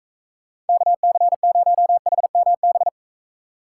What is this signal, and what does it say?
Morse code, heard as a keyed tone, about 42 wpm, 700 Hz. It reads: KC0HMB